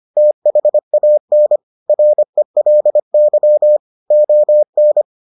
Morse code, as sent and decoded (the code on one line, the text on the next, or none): - .... .- -. / .-. .-.. -.-- / --- -.
THAN RELY ON